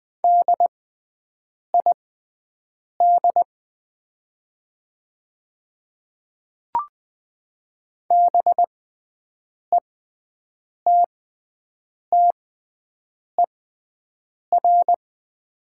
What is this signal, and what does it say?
Morse code, heard as a keyed tone, about 20 wpm, spaced as for 7 wpm, 700 Hz. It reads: DID BETTER
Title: 6x Character Spacing - Top 500 Words - 20wpm